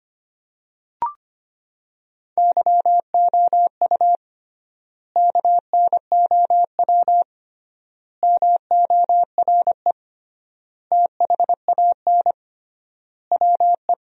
Morse code, heard as a keyed tone, about 25 wpm, 700 Hz. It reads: YOU KNOW MORE THAN WE
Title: 3x Word-Spacing - Sentences from Top 100 Words - 25wpm